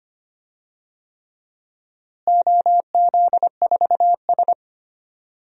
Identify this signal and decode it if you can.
Morse code, heard as a keyed tone, about 25 wpm, 700 Hz. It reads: OZ4S
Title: DXCC-100 Call Signs - 4 Characters - Repeated 3x - 25wpm